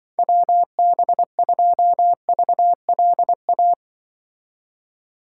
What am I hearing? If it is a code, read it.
Morse code, heard as a keyed tone, about 24 wpm, 700 Hz. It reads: WB2VLA
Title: Speed Racing - USA 2x3 Call Signs 20wpm